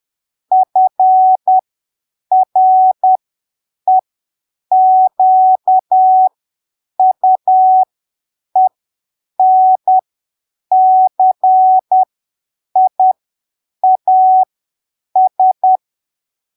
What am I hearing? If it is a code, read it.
Morse code, heard as a keyed tone, about 10 wpm, spaced as for 7 wpm, 750 Hz. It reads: FREQUENCIAS
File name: arquivos / texto / texto 07 10ppm